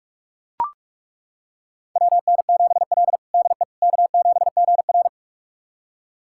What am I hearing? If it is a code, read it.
Morse code, heard as a keyed tone, about 41 wpm, 700 Hz. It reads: WN7L DE K7GR
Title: USA Callsign DE Callsign - 4 Characters - 45wpm